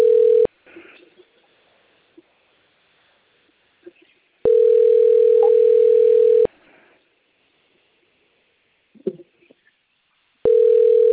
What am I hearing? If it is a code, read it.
Ringback tone (North America), 440 + 480 Hz.